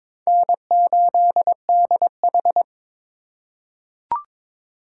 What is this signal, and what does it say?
Morse code, heard as a keyed tone, about 22 wpm, 700 Hz. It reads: N8DH
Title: USA Call Signs - 4 Characters - 3x Character Spacing - 22wpm